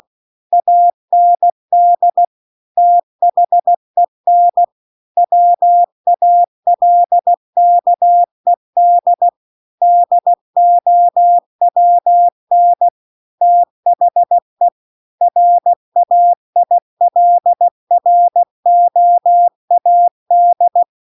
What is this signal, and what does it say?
Morse code, heard as a keyed tone, about 16 wpm, 700 Hz. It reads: AND THEN WALKED DOWN THE RAILROAD
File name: CW-The-Big-Twoheaded-River-1-16WPM